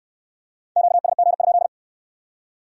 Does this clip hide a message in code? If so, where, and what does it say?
Yes: Morse code from 0.8 to 1.7 s, heard as a keyed tone, about 66 wpm, 700 Hz. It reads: QSG?